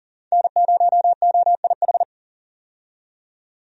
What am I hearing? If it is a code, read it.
Morse code, heard as a keyed tone, about 40 wpm, 700 Hz. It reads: N0OIH
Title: USA Call Signs - 5 Characters - 40wpm